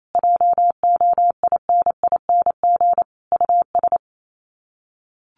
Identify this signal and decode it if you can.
Morse code, heard as a keyed tone, about 28 wpm, 700 Hz. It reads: JOINING US
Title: Sets of 5 Words 28wpm